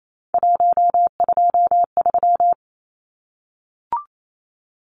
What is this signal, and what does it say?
Morse code, heard as a keyed tone, about 28 wpm, 700 Hz. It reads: 123
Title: Numbers - 3-Digit 28wpm